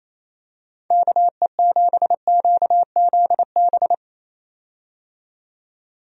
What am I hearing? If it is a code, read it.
Morse code, heard as a keyed tone, about 28 wpm, 700 Hz. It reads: KE7QZB
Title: USA Call Signs - 6 Characters - 28wpm